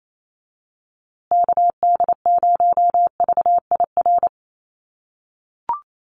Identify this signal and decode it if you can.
Morse code, heard as a keyed tone, about 28 wpm, 700 Hz. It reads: KD0VIR
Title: USA Call Signs Repeated 3x 28wpm